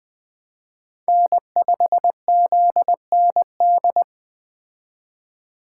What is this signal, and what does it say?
Morse code, heard as a keyed tone, about 20 wpm, 700 Hz. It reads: N5ZND